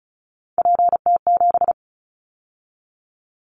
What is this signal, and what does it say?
Morse code, heard as a keyed tone, about 34 wpm, 700 Hz. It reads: PT7
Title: Callsign Prefixes for Top 100 DXCC - 35wpm